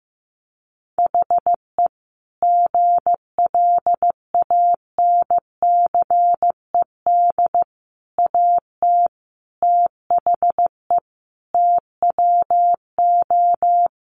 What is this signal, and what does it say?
Morse code, heard as a keyed tone, about 15 wpm, 700 Hz. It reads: HE GLANCED AT THE TWO
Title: A Sound Of Thunder 15wpm